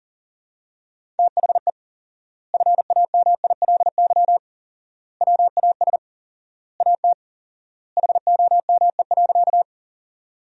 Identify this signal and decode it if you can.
Morse code, heard as a keyed tone, about 40 wpm, 700 Hz. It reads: THE FAMILY WAS AT HOME.
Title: 4x Word-Spacing - Sentences from Top 400 Words - 40wpm